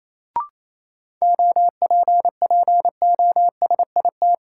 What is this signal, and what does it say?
Morse code, heard as a keyed tone, about 28 wpm, 700 Hz. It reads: OPPOSIT
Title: Sentences from Top 1300 Words - New Word Drill Alt - 28wpm